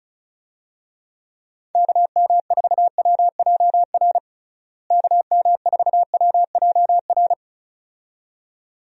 Morse code, heard as a keyed tone, about 36 wpm, 700 Hz. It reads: KM4WJR KM4WJR
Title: USA Call Signs - 6 Characters - Repeated 2x - 35wpm